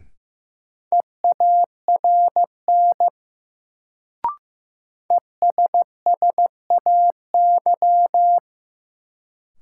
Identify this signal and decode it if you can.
Morse code, heard as a keyed tone, about 16 wpm, 700 Hz. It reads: EARN ESSAY
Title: Sentences from Top 1500 Words - New Word Drill - 15wpm